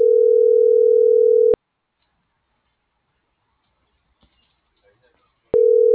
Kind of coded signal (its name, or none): telephone call progress tone